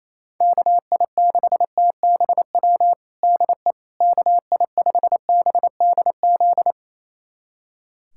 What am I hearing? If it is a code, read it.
Morse code, heard as a keyed tone, about 28 wpm, 700 Hz. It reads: KI6TBW DE KI5BDZ
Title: USA Callsign DE Callsign - 6 Characters - 28wpm